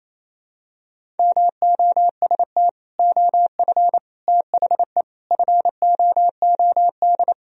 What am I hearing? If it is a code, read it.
Morse code, heard as a keyed tone, about 28 wpm, 700 Hz. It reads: MOST OF THE FOOD
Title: Sentences from Top 1000 Words - Review Part 1 - 28wpm